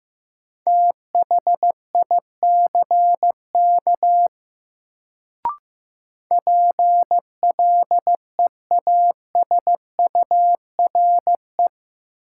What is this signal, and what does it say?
Morse code, heard as a keyed tone, about 15 wpm, 700 Hz. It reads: THICK PLEASURE
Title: Sentences from Top 1600 Words - New Word Drill Alt - 15wpm